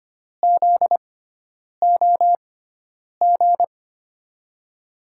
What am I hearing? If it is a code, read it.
Morse code, heard as a keyed tone, about 25 wpm, spaced as for 9 wpm, 700 Hz. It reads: ZOG
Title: USA Call Signs - 6 Characters - 6x Character Spacing - 25wpm